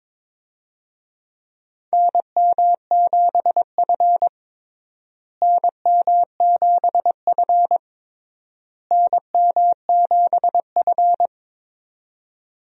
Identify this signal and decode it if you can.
Morse code, heard as a keyed tone, about 22 wpm, 700 Hz. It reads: NM7F NM7F NM7F
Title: USA Call Signs - 4 Characters - Repeated 3x - 22wpm